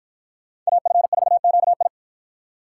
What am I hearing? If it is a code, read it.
Morse code, heard as a keyed tone, about 53 wpm, 700 Hz. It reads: AF4QI